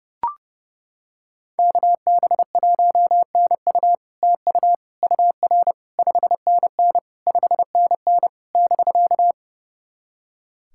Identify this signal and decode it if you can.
Morse code, heard as a keyed tone, about 28 wpm, 700 Hz. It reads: KB1NU TU UR 5NN 5NN <BK>
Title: SOTA QSO Practice 30wpm